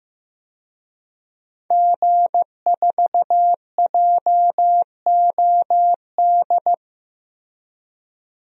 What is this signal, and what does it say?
Morse code, heard as a keyed tone, about 15 wpm, 700 Hz. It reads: G4JOD
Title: DXCC-100 Call Signs - 5 Characters - Repeated 3x - 15wpm